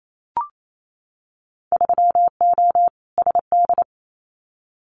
Morse code, heard as a keyed tone, about 28 wpm, 700 Hz. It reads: 3O SD
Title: Winter Field Day Exchanges 28wpm